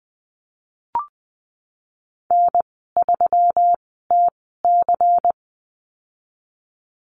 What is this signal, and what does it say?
Morse code, heard as a keyed tone, about 20 wpm, spaced as for 15 wpm, 700 Hz. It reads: N3TC
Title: 147 Lesson 24g - Letter y - Call Signs 20wpm